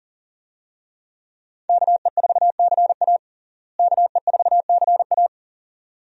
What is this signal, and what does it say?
Morse code, heard as a keyed tone, about 40 wpm, 700 Hz. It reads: KE4CA KE4CA
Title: USA Call Signs - 5 Characters - Repeated 2x - 40wpm